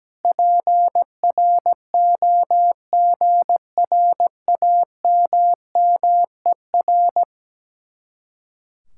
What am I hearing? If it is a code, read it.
Morse code, heard as a keyed tone, about 17 wpm, 700 Hz. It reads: PROGRAMMER
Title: Speed-Racing Common QSO Phrases - 17wpm